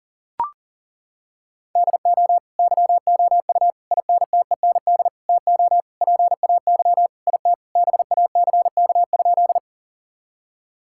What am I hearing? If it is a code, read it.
Morse code, heard as a keyed tone, about 37 wpm, 700 Hz. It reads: DO YOU INTEND TO PAY IT BACK?
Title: Sentences from Top 1600 Words 40wpm